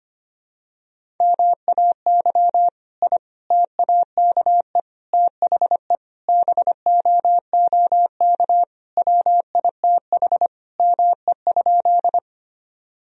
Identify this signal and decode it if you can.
Morse code, heard as a keyed tone, about 25 wpm, 700 Hz. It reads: MAY I TAKE THE BOOK WITH ME?
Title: Sentences from Top 1000 Words - Review Part 1 - 25wpm